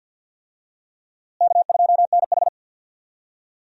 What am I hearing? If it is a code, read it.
Morse code, heard as a keyed tone, about 50 wpm, 700 Hz. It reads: K2NH
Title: USA Call Signs - 4 Characters - Repeated 3x - 50wpm